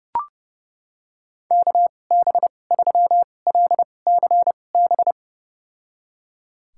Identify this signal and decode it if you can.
Morse code, heard as a keyed tone, about 30 wpm, spaced as for 22 wpm, 700 Hz. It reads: KB3LCB
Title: USA Call Signs - 6 Characters - 2x Character Spacing - 30wpm